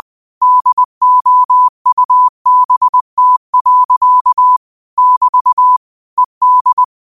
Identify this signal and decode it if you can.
Morse code, heard as a keyed tone, about 20 wpm, 1000 Hz. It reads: DOUBT. <BT> ED
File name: qotdfiles / 20wpm